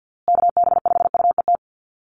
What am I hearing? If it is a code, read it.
Morse code, heard as a keyed tone, about 50 wpm, 700 Hz. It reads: KB5UET
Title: USA Call Signs Repeated 3x 50wpm